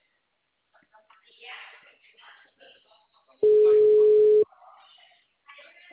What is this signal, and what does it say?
Ringback tone (Europe, 425 Hz), with speech.